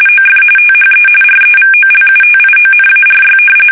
RTTY, about 45.45 baud. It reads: OF THE REVOLUTIONARY
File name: fsk850 / usnavy1